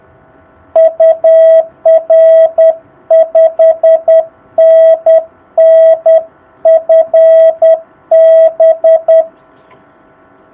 Morse code, heard as a keyed tone, about 10 wpm, 660 Hz. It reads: UR 5NN FB
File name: jh8cbh / 2019cw